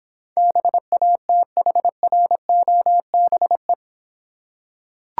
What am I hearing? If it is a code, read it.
Morse code, heard as a keyed tone, about 24 wpm, 700 Hz. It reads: BATHROBE
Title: Difficult Words - Speed-Racing with Hints - 20wpm